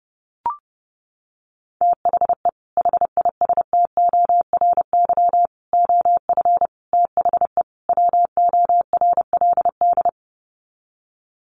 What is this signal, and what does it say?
Morse code, heard as a keyed tone, about 30 wpm, 700 Hz. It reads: THE HISTORY OF THE WORLD
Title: Sets of 5 Words Encore2 30wpm